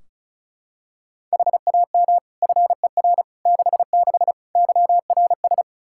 Morse code, heard as a keyed tone, about 36 wpm, 700 Hz. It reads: HAM FER 66 YRS
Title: K1USN SST QSO Practice 35wpm